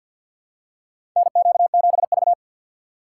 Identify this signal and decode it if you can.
Morse code, heard as a keyed tone, about 46 wpm, 700 Hz. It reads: NQ7V
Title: USA Call Signs - 4 Characters - 2x Character Spacing - 50wpm